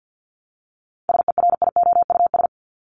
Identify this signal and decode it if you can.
Morse code, heard as a keyed tone, about 50 wpm, 700 Hz. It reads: SERIOUS
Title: LY Suffix - Top 100 Words with Base Words 50wpm